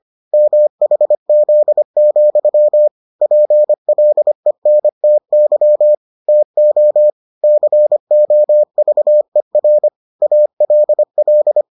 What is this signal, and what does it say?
Morse code, heard as a keyed tone, about 25 wpm, 600 Hz. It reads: MHZ, PLENTY TO COVER ALL